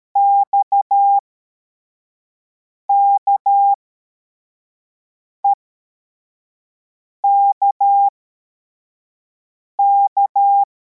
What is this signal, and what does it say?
Morse code, heard as a keyed tone, about 13 wpm, spaced as for 4 wpm, 800 Hz. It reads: XKEKK
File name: formation / cw / Text21